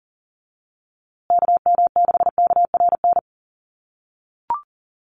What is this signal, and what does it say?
Morse code, heard as a keyed tone, about 40 wpm, 700 Hz. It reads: KM6KRN
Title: USA Call Signs Repeated 2x 40wpm